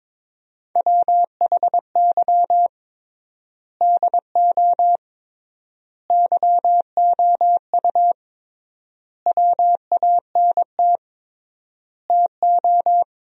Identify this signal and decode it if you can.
Morse code, heard as a keyed tone, about 22 wpm, 700 Hz. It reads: WHY DO YOU WANT TO
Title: 3x Word-Spacing - Sentences from Top 200 Words - 22wpm